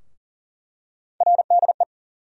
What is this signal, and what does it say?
Morse code, heard as a keyed tone, about 40 wpm, 700 Hz. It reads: RDE
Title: USA Call Sign Postfixes - 3 Letters 40wpm